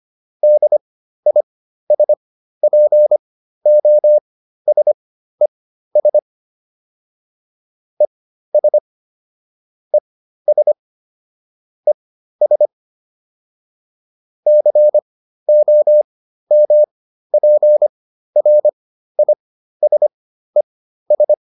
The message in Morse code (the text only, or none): DISPOSES ES ES ES COMPRISES